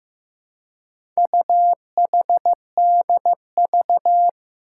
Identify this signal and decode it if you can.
Morse code, heard as a keyed tone, about 15 wpm, 700 Hz. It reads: UHDV
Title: Four Challenging Letter-Number Combinations 15wpm